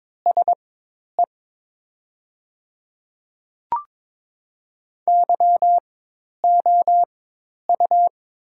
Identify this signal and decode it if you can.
Morse code, heard as a keyed tone, about 22 wpm, spaced as for 10 wpm, 700 Hz. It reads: SE YOU
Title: 4x Character Spacing - Top 300 Words - 22wpm